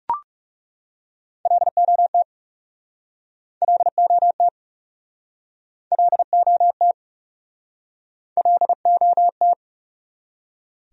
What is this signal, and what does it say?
Morse code, heard as a keyed tone, about 43 wpm, 700 Hz. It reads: LOT LOT LOT LOT